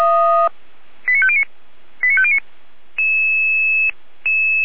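Five-tone selective call (ZVEI), heard as quick runs of five tones, tones of about 70 ms. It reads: 89509 89509
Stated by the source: ZVEI five-tone call 89305 received by SDR on channel 455, Feuerwehr Orlamünde fire brigade.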